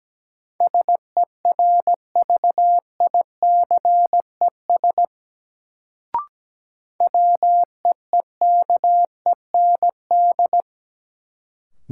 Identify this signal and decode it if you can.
Morse code, heard as a keyed tone, about 17 wpm, 700 Hz. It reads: SERVICES WEEKEND
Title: Sets of 5 Words - New Word Drill - 17wpm